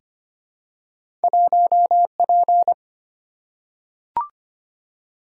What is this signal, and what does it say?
Morse code, heard as a keyed tone, about 25 wpm, 700 Hz. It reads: 1P
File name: Two Letter-Number Combinations - 25wpm